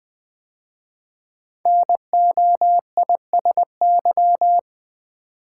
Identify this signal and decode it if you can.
Morse code, heard as a keyed tone, about 20 wpm, 700 Hz. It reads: NOISY
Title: 5 Letter Words - First 3 Letters Given - 20wpm